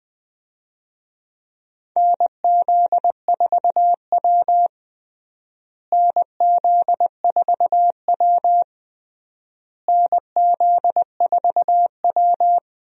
Morse code, heard as a keyed tone, about 20 wpm, 700 Hz. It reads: NZ4W NZ4W NZ4W
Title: USA Call Signs - 4 Characters - Repeated 3x - 20wpm